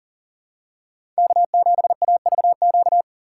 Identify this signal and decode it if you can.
Morse code, heard as a keyed tone, about 38 wpm, 700 Hz. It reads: K7AVQ
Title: USA Call Signs - 5 Characters - 6x Character Spacing - 40wpm